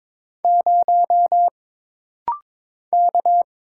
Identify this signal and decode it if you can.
Morse code, heard as a keyed tone, about 22 wpm, 700 Hz. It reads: 0 K
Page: This rapid-fire timing is geared to push your ICR (Instant Character Recognition) skills.